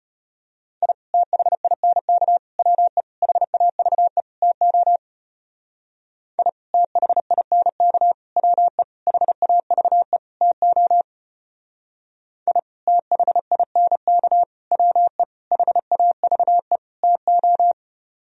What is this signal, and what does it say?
Morse code, heard as a keyed tone, about 29 wpm, 700 Hz. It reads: I THINK WE HAVE TO I THINK WE HAVE TO I THINK WE HAVE TO
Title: Speed-Racing - Sets of 5 Words 25wpm